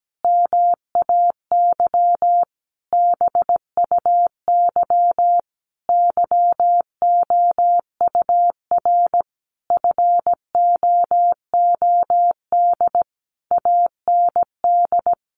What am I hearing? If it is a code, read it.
Morse code, heard as a keyed tone, about 17 wpm, 700 Hz. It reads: MAY BUY YOUR FOOD AND
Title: Sentences from Top 800 Words 17wpm